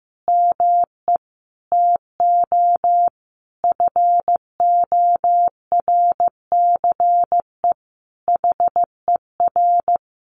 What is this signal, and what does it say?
Morse code, heard as a keyed tone, about 15 wpm, 700 Hz. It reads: ME TO FORCE HER
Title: Sentences from Top 1500 Words 15wpm